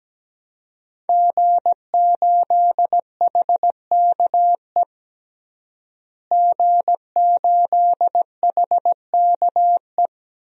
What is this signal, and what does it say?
Morse code, heard as a keyed tone, about 17 wpm, 700 Hz. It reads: G8HKE G8HKE
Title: DXCC-100 Call Signs - 5 Characters - Repeated 3x - 17wpm